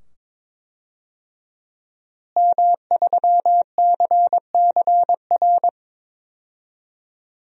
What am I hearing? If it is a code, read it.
Morse code, heard as a keyed tone, about 22 wpm, 700 Hz. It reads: M3CCR
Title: DXCC-100 Call Signs - 5 Characters - Repeated 2x - 22wpm